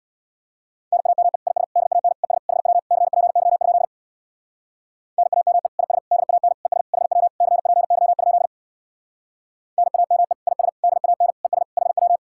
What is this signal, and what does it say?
Morse code, heard as a keyed tone, about 62 wpm, 700 Hz. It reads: NAME IS DAN ES HW CPY? NAME IS DAN ES HW CPY? NAME IS DAN ES HW
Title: Speed-Racing Common QSO Phrases Encore - 50wpm